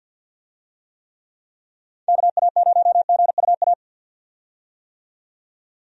Morse code, heard as a keyed tone, about 51 wpm, 700 Hz. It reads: KA0GUA
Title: USA Call Signs - 6 Characters - 50wpm